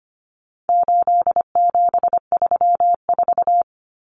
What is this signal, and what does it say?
Morse code, heard as a keyed tone, about 25 wpm, 700 Hz. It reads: 8734
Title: Common Cut Numbers - 4-Digit 25wpm